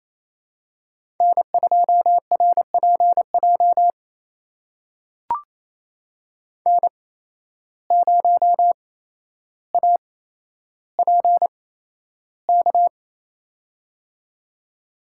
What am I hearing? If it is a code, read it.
Morse code, heard as a keyed tone, about 28 wpm, spaced as for 12 wpm, 700 Hz. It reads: N2RPJ N0APK
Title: USA Call Signs - 5 Characters - 8x Character Spacing - 28wpm